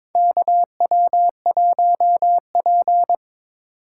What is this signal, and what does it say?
Morse code, heard as a keyed tone, about 22 wpm, 700 Hz. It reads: KW1P
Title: USA Call Signs - 4 Characters - 4x Character Spacing - 22wpm